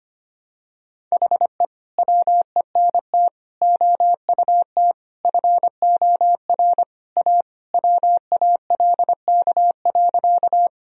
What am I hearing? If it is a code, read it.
Morse code, heard as a keyed tone, about 25 wpm, 700 Hz. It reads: HE WENT OUT FOR A WALK.
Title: Sentences from Top 1000 Words - Review Part 1 - 25wpm